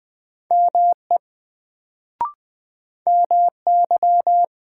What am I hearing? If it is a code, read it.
Morse code, heard as a keyed tone, about 20 wpm, 700 Hz. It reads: ME MY
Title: Sets of 2 Words - New Word Drill - 20wpm